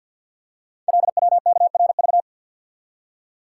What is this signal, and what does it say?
Morse code, heard as a keyed tone, about 50 wpm, 690 Hz. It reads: LWKRV
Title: Code Groups - 5 Letters - 50wpm